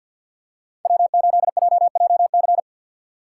Morse code, heard as a keyed tone, about 49 wpm, 690 Hz. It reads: W8JJC
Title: USA Call Signs - 5 Characters - 2x Character Spacing - 50wpm